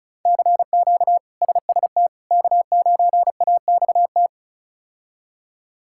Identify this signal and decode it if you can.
Morse code, heard as a keyed tone, about 36 wpm, 700 Hz. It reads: CQ SST K9AXT
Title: K1USN SST QSO Practice - 35wpm